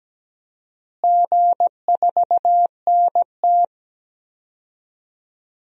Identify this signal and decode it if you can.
Morse code, heard as a keyed tone, about 17 wpm, 700 Hz. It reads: G4NT
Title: DXCC-100 Call Signs - 4 Characters - Repeated 2x - 17wpm